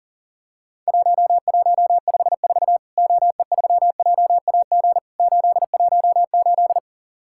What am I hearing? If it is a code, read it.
Morse code, heard as a keyed tone, about 40 wpm, 700 Hz. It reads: 1154 OE3JAG 818
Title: Worked All Europe DX Contest - QTC Exchange - 40wpm